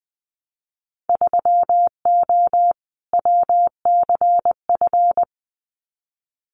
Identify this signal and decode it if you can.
Morse code, heard as a keyed tone, about 20 wpm, 700 Hz. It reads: 3O WCF
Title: Winter Field Day Exchanges 20wpm